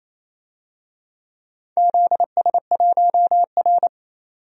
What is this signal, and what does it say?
Morse code, heard as a keyed tone, about 27 wpm, 700 Hz. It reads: ZS1R
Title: DXCC-100 Call Signs - 4 Characters - 28wpm